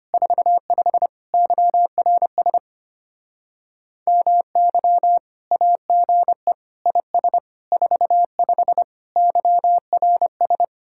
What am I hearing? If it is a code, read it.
Morse code, heard as a keyed tone, about 33 wpm, 700 Hz. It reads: 45 YRS MY AGE IS 45 YRS